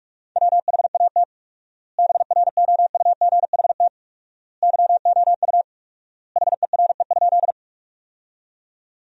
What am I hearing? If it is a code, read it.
Morse code, heard as a keyed tone, about 46 wpm, 690 Hz. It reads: WHAT BROUGHT YOU HERE?
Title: 4x Word-Spacing - Sentences from Top 500 Words - 45wpm